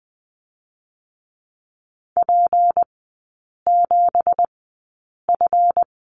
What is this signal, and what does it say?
Morse code, heard as a keyed tone, about 19 wpm, spaced as for 8 wpm, 690 Hz. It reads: P7F
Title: 166 Lesson 27e - Number 7 - Three Characters 20wpm